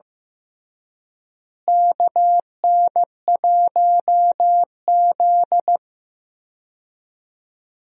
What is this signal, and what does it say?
Morse code, heard as a keyed tone, about 15 wpm, 700 Hz. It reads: KN1Z
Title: USA Call Signs - 4 Characters - Repeated 3x - 15wpm